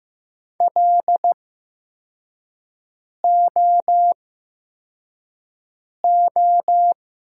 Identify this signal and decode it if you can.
Morse code, heard as a keyed tone, about 15 wpm, spaced as for 4 wpm, 700 Hz. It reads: LOO
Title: 8x Character Spacing - Top 300 Words - 15wpm